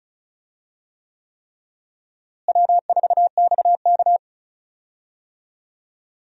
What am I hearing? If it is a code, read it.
Morse code, heard as a keyed tone, about 36 wpm, 700 Hz. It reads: W4XK
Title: USA Call Signs - 4 Characters - Repeated 3x - 35wpm